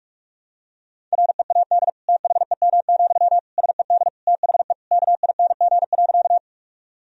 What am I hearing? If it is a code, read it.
Morse code, heard as a keyed tone, about 46 wpm, 700 Hz. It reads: READ THEM, SED THE KING.
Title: Sentences from Top 400 Words and Abbreviations 45wpm